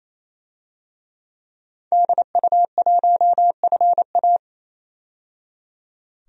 Morse code, heard as a keyed tone, about 27 wpm, 700 Hz. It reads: DU1FA
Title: DXCC-100 Call Signs - 5 Characters - 28wpm